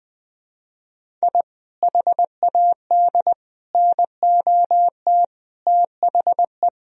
Morse code, heard as a keyed tone, about 19 wpm, 700 Hz. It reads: I HAD NOT THE
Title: Sentences from Top 1000 Words - Review Part 1 - 20wpm